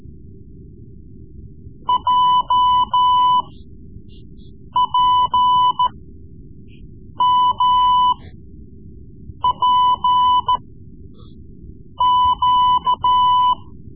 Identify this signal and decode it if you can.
Morse code, heard as a keyed tone, about 14 wpm, spaced as for 5 wpm, 1000 Hz. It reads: JPMPQ
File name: Morse Lessons / Morse Lessons / Tape-3B